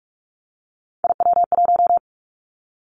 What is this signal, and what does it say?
Morse code, heard as a keyed tone, about 46 wpm, 690 Hz. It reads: IW1
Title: Callsign Prefixes for Top 50 DXCC - 45wpm